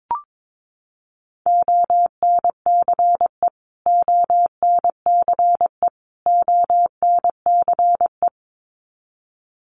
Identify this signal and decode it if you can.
Morse code, heard as a keyed tone, about 22 wpm, 700 Hz. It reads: ONCE ONCE ONCE
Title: Top 500 Words Repeated 22wpm